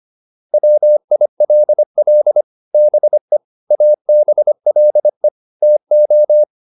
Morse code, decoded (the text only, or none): WILL BE ABLE TO